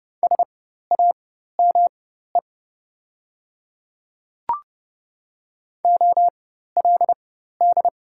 Morse code, heard as a keyed tone, about 30 wpm, spaced as for 14 wpm, 700 Hz. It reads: SAME OLD
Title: 4x Character Spacing - Top 300 Words - 30wpm